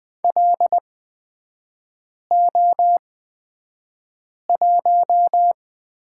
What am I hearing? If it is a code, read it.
Morse code, heard as a keyed tone, about 20 wpm, spaced as for 5 wpm, 700 Hz. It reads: LO1
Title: 236 Lesson 38e - Letter q - Three Characters 20wpm